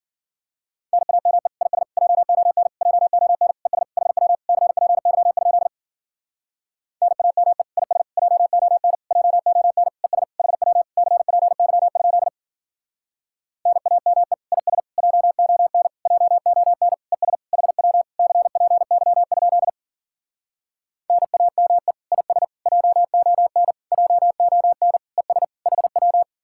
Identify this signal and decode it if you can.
Morse code, heard as a keyed tone, about 52 wpm, 700 Hz. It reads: NAME IS JON JON ES HW CPY? NAME IS JON JON ES HW CPY? NAME IS JON JON ES HW CPY? NAME IS JON JON ES HW